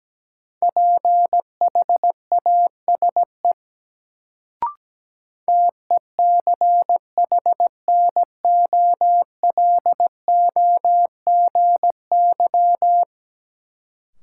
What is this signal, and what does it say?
Morse code, heard as a keyed tone, about 17 wpm, 700 Hz. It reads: PHASE TECHNOLOGY
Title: Sentences from Top 1200 Words - New Word Drill Alt - 17wpm